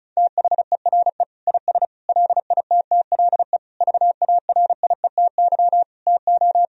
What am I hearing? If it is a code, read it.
Morse code, heard as a keyed tone, about 36 wpm, 700 Hz. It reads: THERE IS LITTLE VARIETY TO
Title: Sentences from Top 1000 Words - Review Part 4 - 35wpm